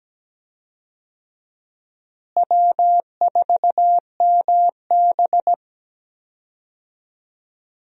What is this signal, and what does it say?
Morse code, heard as a keyed tone, about 17 wpm, 700 Hz. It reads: W4MB
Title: USA Call Signs - 4 Characters - Repeated 2x - 17wpm